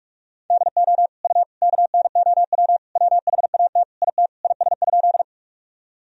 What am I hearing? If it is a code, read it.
Morse code, heard as a keyed tone, about 47 wpm, 700 Hz. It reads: DO U KNOW WHAT IT IS?
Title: Sentences from Top 100 Words and Abbreviations 45wpm